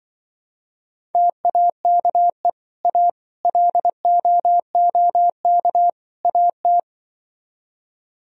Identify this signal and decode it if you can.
Morse code, heard as a keyed tone, about 24 wpm, 700 Hz. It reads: TAKE A LOOK AT
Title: Speed-Racing - Sets of 4 Words 20wpm